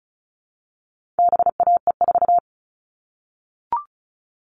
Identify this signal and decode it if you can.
Morse code, heard as a keyed tone, about 34 wpm, 700 Hz. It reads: BAE4
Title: All Cut Numbers - 4-Digit 35wpm